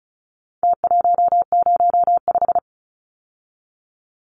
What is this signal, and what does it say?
Morse code, heard as a keyed tone, about 34 wpm, 700 Hz. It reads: T105
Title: Numbers - Contest Style - 4-Digit 35wpm